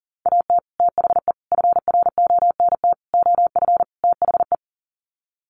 Morse code, heard as a keyed tone, about 40 wpm, 700 Hz. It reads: AT THE FRONT OF THE
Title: Sets of 5 Words 40wpm